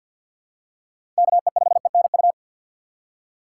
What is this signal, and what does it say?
Morse code, heard as a keyed tone, about 47 wpm, 700 Hz. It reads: KE5ENU
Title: USA Call Signs - 6 Characters - 6x Character Spacing - 50wpm